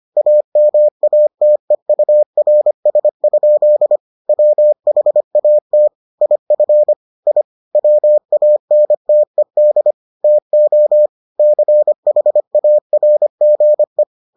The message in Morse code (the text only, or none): AMATEURS? WHAT IF I WANTED TO CHARGE